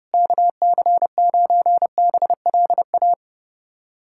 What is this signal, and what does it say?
Morse code, heard as a keyed tone, about 30 wpm, 700 Hz. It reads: KC9BLA
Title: USA Callsign DE Callsign 30wpm